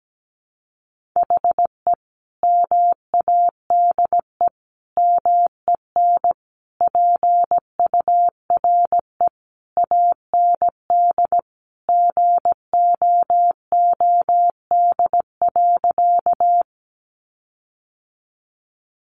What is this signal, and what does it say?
Morse code, heard as a keyed tone, about 17 wpm, 700 Hz. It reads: HE MADE MEN PURE AND GOOD.
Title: Sentences from Top 1500 Words 17wpm